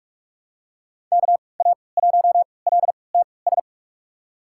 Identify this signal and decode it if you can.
Morse code, heard as a keyed tone, about 42 wpm, spaced as for 25 wpm, 700 Hz. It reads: KA1LTS
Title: USA Call Signs - 6 Characters - 3x Character Spacing - 45wpm